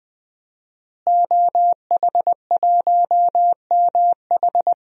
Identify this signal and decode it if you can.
Morse code, heard as a keyed tone, about 20 wpm, 700 Hz. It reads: OH1MH
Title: DXCC-100 Call Signs - 5 Characters - Repeated 2x - 20wpm